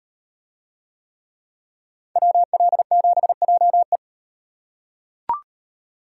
Morse code, heard as a keyed tone, about 36 wpm, 700 Hz. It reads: WL7JE